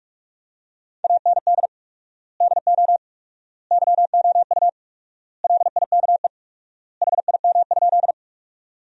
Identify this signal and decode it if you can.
Morse code, heard as a keyed tone, about 44 wpm, 700 Hz. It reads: AND DO YOU LIKE HIM?
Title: 4x Word-Spacing - Sentences from Top 100 Words - 45wpm